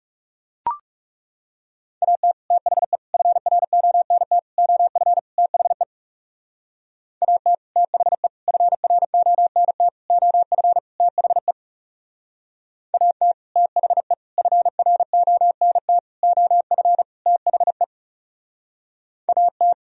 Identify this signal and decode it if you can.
Morse code, heard as a keyed tone, about 35 wpm, 700 Hz. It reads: AT THE FRONT OF THE AT THE FRONT OF THE AT THE FRONT OF THE AT
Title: Speed-Racing - Sets of 5 Words 30wpm